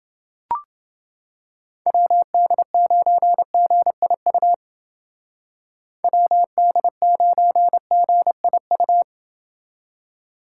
Morse code, heard as a keyed tone, about 29 wpm, 700 Hz. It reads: WD9GIU WD9GIU